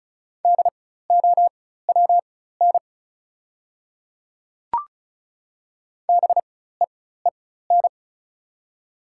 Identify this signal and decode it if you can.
Morse code, heard as a keyed tone, about 36 wpm, spaced as for 16 wpm, 690 Hz. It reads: DOWN BEEN
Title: 4x Character Spacing - Top 100 Words - 35wpm